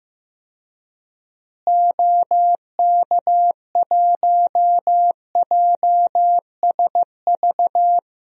Morse code, heard as a keyed tone, about 15 wpm, 700 Hz. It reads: OK1JSV
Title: DXCC-100 Call Signs - 6 Characters - Repeated 2x - 15wpm